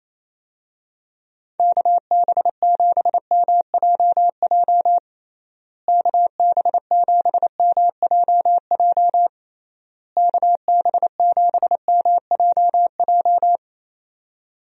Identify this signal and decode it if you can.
Morse code, heard as a keyed tone, about 28 wpm, 700 Hz. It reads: KB7MJJ KB7MJJ KB7MJJ
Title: USA Call Signs - 6 Characters - Repeated 3x - 28wpm